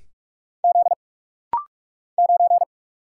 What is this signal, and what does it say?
Morse code, heard as a keyed tone, about 44 wpm, 700 Hz. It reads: Z 9
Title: Single Letter-Number Alt - Rapid-Fire 45wpm